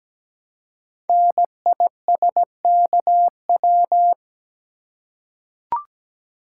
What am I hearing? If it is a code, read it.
Morse code, heard as a keyed tone, about 17 wpm, 700 Hz. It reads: NISKW
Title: Code Groups - 5 Letters - 17wpm